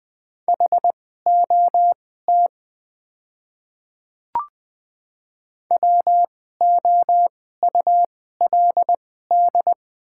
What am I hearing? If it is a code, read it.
Morse code, heard as a keyed tone, about 20 wpm, spaced as for 14 wpm, 690 Hz. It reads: HOT WOULD
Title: 2x Character Spacing - Top 100 Words - 20wpm